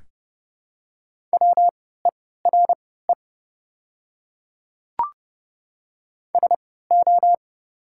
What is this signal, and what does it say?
Morse code, heard as a keyed tone, about 31 wpm, spaced as for 17 wpm, 700 Hz. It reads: WERE SO
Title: 3x Character Spacing - Top 100 Words - 30wpm